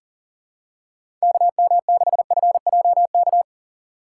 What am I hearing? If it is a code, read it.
Morse code, heard as a keyed tone, about 40 wpm, 700 Hz. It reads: KM6FJK
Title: USA Call Signs - 6 Characters - 2x Character Spacing - 40wpm